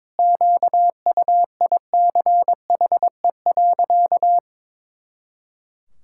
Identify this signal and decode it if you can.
Morse code, heard as a keyed tone, about 22 wpm, 700 Hz. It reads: QUICHE.
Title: Pangram Challenge v2 - 22wpm